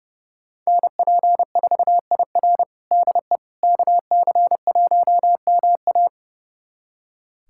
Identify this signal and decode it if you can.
Morse code, heard as a keyed tone, about 29 wpm, 700 Hz. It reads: NP4IR DE KC1MA
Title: USA Callsign DE Callsign - 5 Characters - 30wpm